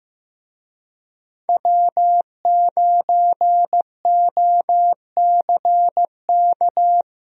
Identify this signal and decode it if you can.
Morse code, heard as a keyed tone, about 15 wpm, 700 Hz. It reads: W9OCK
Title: USA Call Signs - 5 Characters - Repeated 2x - 15wpm